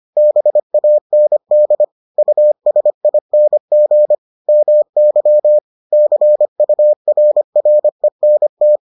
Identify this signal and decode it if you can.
Morse code, heard as a keyed tone, about 25 wpm, 600 Hz. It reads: BAND USING MY CURRENT